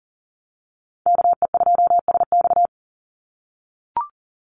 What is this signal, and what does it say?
Morse code, heard as a keyed tone, about 40 wpm, 700 Hz. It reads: KE2SX
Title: Call Signs - 40wpm